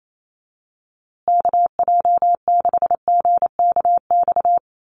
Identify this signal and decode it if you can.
Morse code, heard as a keyed tone, about 28 wpm, 700 Hz. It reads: KJ6GKX
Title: USA Call Signs Repeated 3x 28wpm